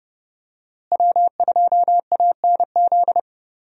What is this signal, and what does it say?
Morse code, heard as a keyed tone, about 29 wpm, 700 Hz. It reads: W2ANZ
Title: USA Call Signs - 5 Characters - Repeated 3x - 30wpm